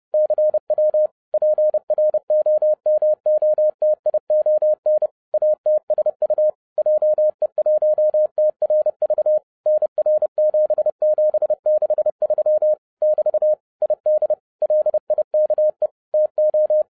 Morse code, heard as a keyed tone, about 30 wpm, 610 Hz. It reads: CW PROMOTION ATSU JE1TRV NR7763 <BT> ID LIKE TO